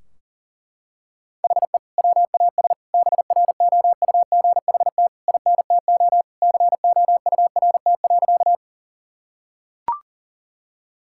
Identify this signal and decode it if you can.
Morse code, heard as a keyed tone, about 40 wpm, 700 Hz. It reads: HE WAS BROUGHT INTO COURT.
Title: Sentences from Top 1600 Words 40wpm